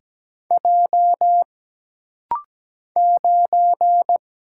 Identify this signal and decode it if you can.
Morse code, heard as a keyed tone, about 17 wpm, 700 Hz. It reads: J 9
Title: Single Letter-Number Alt - Rapid-Fire 17wpm